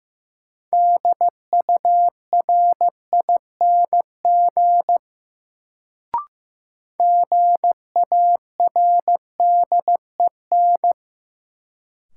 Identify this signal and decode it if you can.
Morse code, heard as a keyed tone, about 15 wpm, 700 Hz. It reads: DURING GARDEN
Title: Sentences from Top 700 Words - New Word Drill - 15wpm